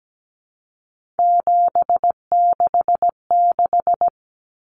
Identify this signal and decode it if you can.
Morse code, heard as a keyed tone, about 17 wpm, 700 Hz. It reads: 766
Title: Common Cut Numbers - 3-Digit 17wpm